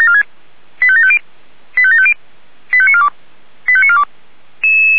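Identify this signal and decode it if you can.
Five-tone selective call (ZVEI), heard as quick runs of five tones, tones of about 70 ms. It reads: 85750 85750 85942 85942
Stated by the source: ZVEI five-tone selective call received with an SDR.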